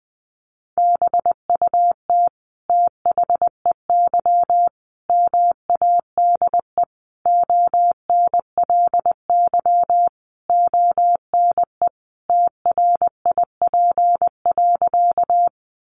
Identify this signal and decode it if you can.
Morse code, heard as a keyed tone, about 20 wpm, 700 Hz. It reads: BUT THEY MADE ONLY ONE TRIP.
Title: Sentences from Top 1300 Words 20wpm